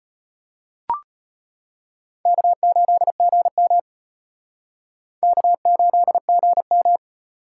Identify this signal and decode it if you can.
Morse code, heard as a keyed tone, about 31 wpm, 700 Hz. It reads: K8GM K8GM
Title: Speed Racing - USA 1x2 Call Signs 25wpm